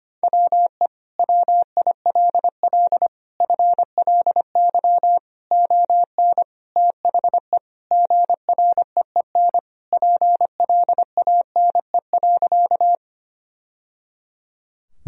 Morse code, heard as a keyed tone, about 25 wpm, 700 Hz. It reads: WE WILL FLY ON THE GREEN PLANE.
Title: Sentences from Top 1000 Words - Review Part 3 - 25wpm